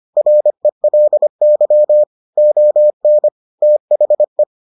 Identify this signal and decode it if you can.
Morse code, heard as a keyed tone, about 25 wpm, 600 Hz. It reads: RELY ON THE